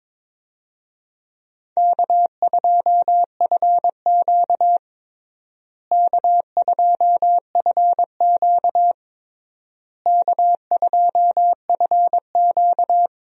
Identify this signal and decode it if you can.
Morse code, heard as a keyed tone, about 22 wpm, 700 Hz. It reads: K2FQ K2FQ K2FQ
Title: USA Call Signs - 4 Characters - Repeated 3x - 22wpm